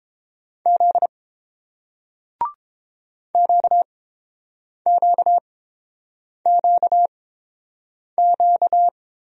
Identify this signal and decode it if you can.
Morse code, heard as a keyed tone, about 30 wpm, spaced as for 7 wpm, 690 Hz. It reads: Z QQQQ